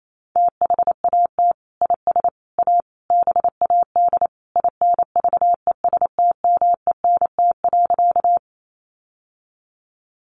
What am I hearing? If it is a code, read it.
Morse code, heard as a keyed tone, about 28 wpm, 700 Hz. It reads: THAT IS A BAD INVESTMENT.
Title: Sentences from Top 700 Words 28wpm